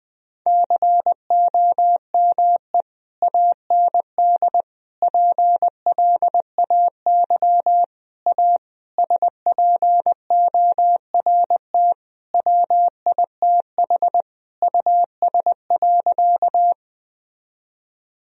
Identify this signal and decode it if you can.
Morse code, heard as a keyed tone, about 20 wpm, 700 Hz. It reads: COME AND PLAY A SPORT WITH US.
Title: Sentences from Top 1000 Words - Review Part 4 - 20wpm